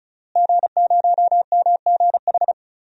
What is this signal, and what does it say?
Morse code, heard as a keyed tone, about 36 wpm, 700 Hz. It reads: G0MGH
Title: DXCC-100 Call Signs - 5 Characters - Repeated 2x - 35wpm